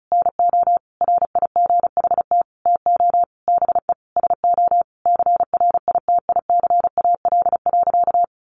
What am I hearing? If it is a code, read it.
Morse code, heard as a keyed tone, about 34 wpm, 700 Hz. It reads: NO RIGHT TO BE SO CRITICAL.
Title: Sentences from Top 1100 Words 35wpm